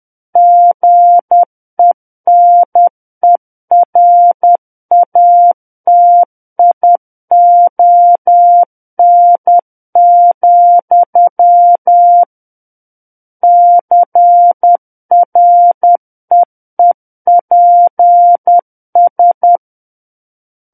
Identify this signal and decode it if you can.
Morse code, heard as a keyed tone, about 10 wpm, 700 Hz. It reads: GENERATION, CREEPS
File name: Book One / 01-Chapter 1 - 10 WPM